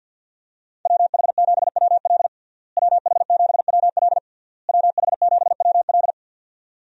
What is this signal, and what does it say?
Morse code, heard as a keyed tone, about 49 wpm, 700 Hz. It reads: WH7WL WH7WL WH7WL
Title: USA Call Signs - 5 Characters - Repeated 3x - 50wpm